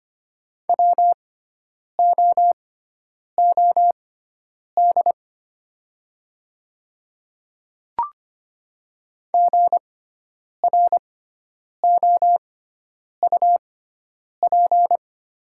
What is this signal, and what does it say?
Morse code, heard as a keyed tone, about 25 wpm, spaced as for 9 wpm, 700 Hz. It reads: WOOD GROUP
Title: 6x Character Spacing - Top 300 Words - 25wpm